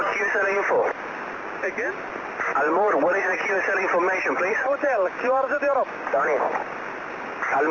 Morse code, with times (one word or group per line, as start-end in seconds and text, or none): none